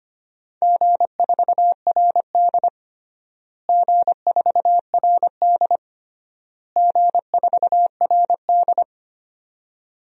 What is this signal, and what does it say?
Morse code, heard as a keyed tone, about 25 wpm, 700 Hz. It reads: G4RD G4RD G4RD
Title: DXCC-100 Call Signs - 4 Characters - Repeated 3x - 25wpm